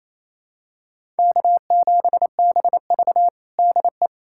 Morse code, heard as a keyed tone, about 27 wpm, 700 Hz. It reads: K7BV DE
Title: Practice QSO - Exchange 4 - Ending - 28wpm